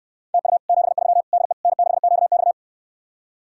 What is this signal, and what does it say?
Morse code, heard as a keyed tone, about 68 wpm, 700 Hz. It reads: TU 73 DE N6QX